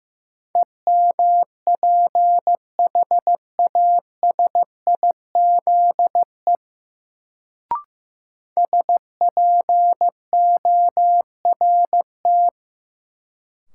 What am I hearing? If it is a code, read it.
Morse code, heard as a keyed tone, about 15 wpm, 700 Hz. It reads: EMPHASIZE SPORT
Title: Sentences from Top 1500 Words - New Word Drill Alt - 15wpm